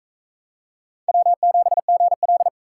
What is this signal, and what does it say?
Morse code, heard as a keyed tone, about 40 wpm, 700 Hz. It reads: W7GL